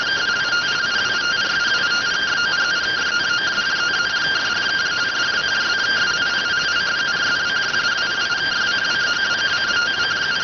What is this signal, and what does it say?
Two-tone teleprinter signal, about 170 Hz shift.